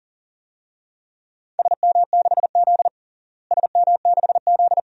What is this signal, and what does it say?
Morse code, heard as a keyed tone, about 40 wpm, 700 Hz. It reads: SM6Z SM6Z
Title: DXCC-100 Call Signs - 4 Characters - Repeated 2x - 40wpm